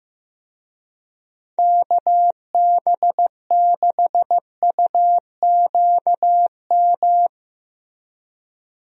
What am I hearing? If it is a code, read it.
Morse code, heard as a keyed tone, about 15 wpm, 700 Hz. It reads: KB6UQM
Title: USA Call Signs - 6 Characters - Repeated 2x - 15wpm